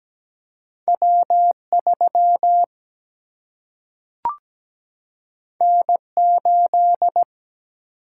Morse code, heard as a keyed tone, about 17 wpm, 700 Hz. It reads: W3 N8
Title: Single-Letter Single-Number 17wpm